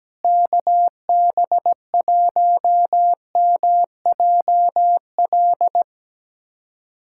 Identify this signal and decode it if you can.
Morse code, heard as a keyed tone, about 17 wpm, 700 Hz. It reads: KB1MJL
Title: USA Callsign DE Callsign - 6 Characters - 17wpm